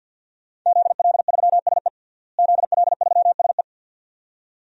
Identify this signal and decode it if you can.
Morse code, heard as a keyed tone, about 52 wpm, 700 Hz. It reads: ZL3SE ZL3SE
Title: DXCC-100 Call Signs - 5 Characters - Repeated 2x - 50wpm